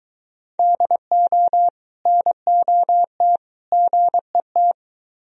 Morse code, heard as a keyed tone, about 22 wpm, 700 Hz. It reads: DO NOT GET